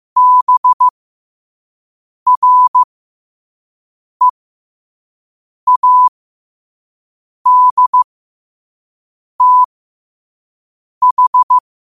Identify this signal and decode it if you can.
Morse code, heard as a keyed tone, about 15 wpm, spaced as for 5 wpm, 1000 Hz. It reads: BREADTH
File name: 5wpm